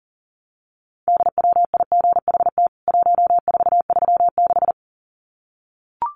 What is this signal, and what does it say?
Morse code, heard as a keyed tone, about 40 wpm, 700 Hz. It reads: DWIGHT 1436
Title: CWT Exchange 40wpm